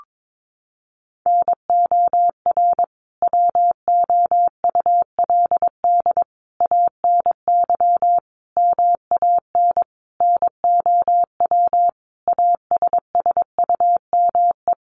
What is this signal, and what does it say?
Morse code, heard as a keyed tone, about 22 wpm, 700 Hz. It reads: NOR WOULD ANY MAN NOW ASSUME
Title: Sentences from Top 1500 Words 22wpm